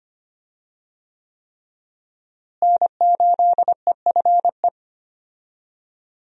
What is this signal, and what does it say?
Morse code, heard as a keyed tone, about 25 wpm, 700 Hz. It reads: N8EFE
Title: USA Call Signs - 5 Characters - Repeated 3x - 25wpm